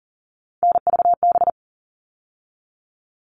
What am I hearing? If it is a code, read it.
Morse code, heard as a keyed tone, about 40 wpm, 700 Hz. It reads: NVB
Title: All Cut Numbers - 3-Digit 40wpm